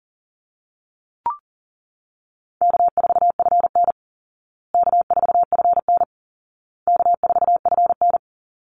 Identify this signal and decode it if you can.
Morse code, heard as a keyed tone, about 40 wpm, 700 Hz. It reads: K4FN K4FN K4FN
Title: Easy Call Signs Repeated 3x - 40wpm